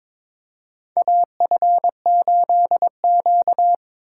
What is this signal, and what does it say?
Morse code, heard as a keyed tone, about 22 wpm, 700 Hz. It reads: AF8Q
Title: USA Call Signs - 2x Character Spacing - 22wpm